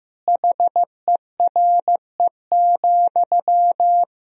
Morse code, heard as a keyed tone, about 15 wpm, 700 Hz. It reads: HERE,
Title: Sentences from Top 1000 Words - Review Part 4 - 15wpm